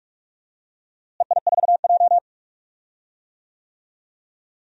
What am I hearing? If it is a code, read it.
Morse code, heard as a keyed tone, about 41 wpm, 700 Hz. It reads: EI4J
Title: DXCC-100 Call Signs - 4 Characters - Repeated 3x - 45wpm